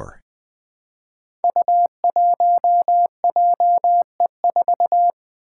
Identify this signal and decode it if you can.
Morse code, heard as a keyed tone, about 20 wpm, 700 Hz. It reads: U1JE4
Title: Five Challenging Letter-Number Combinations 20wpm